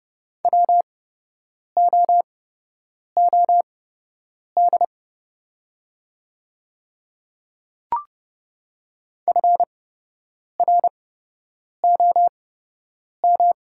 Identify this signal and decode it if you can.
Morse code, heard as a keyed tone, about 30 wpm, spaced as for 8 wpm, 690 Hz. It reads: WOOD FROM
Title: 8x Character Spacing - Top 500 Words - 30wpm